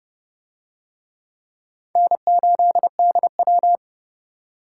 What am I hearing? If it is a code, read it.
Morse code, heard as a keyed tone, about 30 wpm, 700 Hz. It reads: N8DW